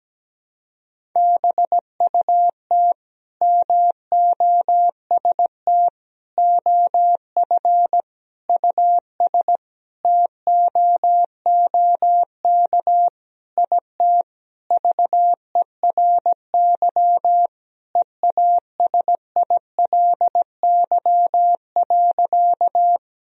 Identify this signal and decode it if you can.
Morse code, heard as a keyed tone, about 17 wpm, 700 Hz. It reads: BUT MOST OF US TOOK IT VERY EASILY.
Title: Sentences from Top 1000 Words - Review Part 1 - 17wpm